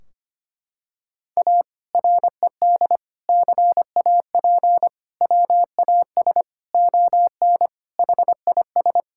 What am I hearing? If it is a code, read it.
Morse code, heard as a keyed tone, about 25 wpm, 700 Hz. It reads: A RED CAP WAS ON HIS
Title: Sentences from Top 1600 Words 25wpm